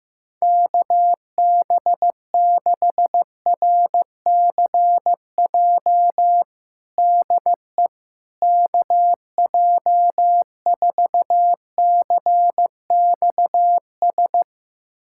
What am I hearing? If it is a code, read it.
Morse code, heard as a keyed tone, about 15 wpm, 700 Hz. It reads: KB6RCJ DE KJ4CXS
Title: USA Callsign DE Callsign - 6 Characters - 15wpm